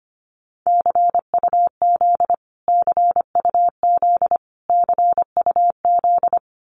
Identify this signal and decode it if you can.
Morse code, heard as a keyed tone, about 25 wpm, 700 Hz. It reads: CUZ CUZ CUZ
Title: Abbreviations Repeated 3x - 25wpm